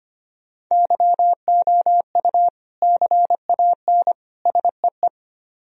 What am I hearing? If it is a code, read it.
Morse code, heard as a keyed tone, about 31 wpm, 700 Hz. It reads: YOU CAN SEE
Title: Speed-Racing - Sets of 3 Words 25wpm